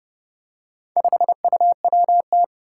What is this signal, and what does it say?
Morse code, heard as a keyed tone, about 29 wpm, 700 Hz. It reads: 5UWT
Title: Four Challenging Letter-Number Combinations 30wpm